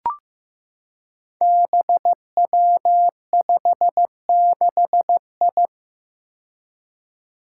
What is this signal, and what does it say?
Morse code, heard as a keyed tone, about 15 wpm, 700 Hz. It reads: BW56I
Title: Five Challenging Letter-Number Combinations 15wpm